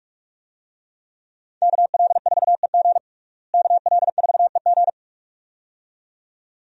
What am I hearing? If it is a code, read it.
Morse code, heard as a keyed tone, about 44 wpm, 700 Hz. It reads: KL4EG KL4EG
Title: USA Call Signs - 5 Characters - Repeated 2x - 45wpm